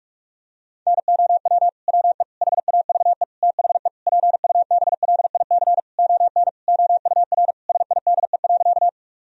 Morse code, heard as a keyed tone, about 46 wpm, 700 Hz. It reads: NOW WE HAVE THE PUBLIC ON OUR SIDE.
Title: Sentences from Top 1000 Words - Review Part 3 - 45wpm